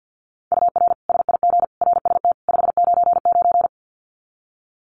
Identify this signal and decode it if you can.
Morse code, heard as a keyed tone, about 50 wpm, 700 Hz. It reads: UR SIG RST 599
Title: Common QSO Phrases 50wpm